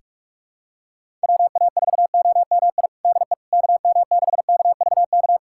Morse code, heard as a keyed tone, about 46 wpm, 700 Hz. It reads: WA4OMI DE KM6KVK
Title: USA Callsign DE Callsign - 6 Characters - 45wpm